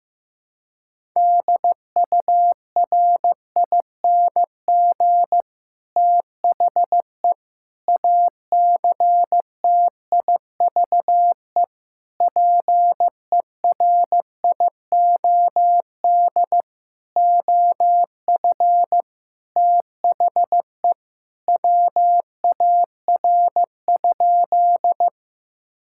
Morse code, heard as a keyed tone, about 15 wpm, 700 Hz. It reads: DURING THE ACTIVE PERIOD OF THE WAR?
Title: Sentences from Top 1000 Words - Review Part 6 - 15wpm